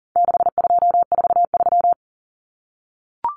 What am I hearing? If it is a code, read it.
Morse code, heard as a keyed tone, about 40 wpm, 700 Hz. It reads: 6243
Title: Numbers - 4-Digit 40wpm